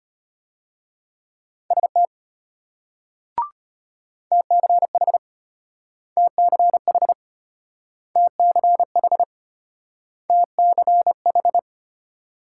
Morse code, heard as a keyed tone, about 33 wpm, 700 Hz. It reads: ST TCH TCH TCH TCH